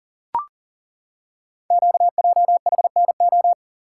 Morse code, heard as a keyed tone, about 43 wpm, 700 Hz. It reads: QJHNO
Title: Code Groups - 5 Letters - 40wpm